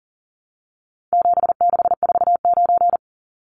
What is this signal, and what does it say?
Morse code, heard as a keyed tone, about 40 wpm, 700 Hz. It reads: 7649
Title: Numbers - 4-Digit 40wpm